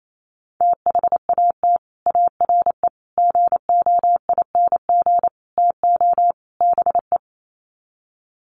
Morse code, heard as a keyed tone, about 28 wpm, 700 Hz. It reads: THAT ARE GOING TO BE
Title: Sets of 5 Words Encore2 28wpm